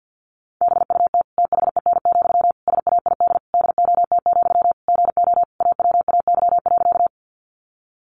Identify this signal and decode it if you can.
Morse code, heard as a keyed tone, about 50 wpm, 700 Hz. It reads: BUT THEN, SAID DOT, GO AWAY.
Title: Sentences from Top 1500 Words 50wpm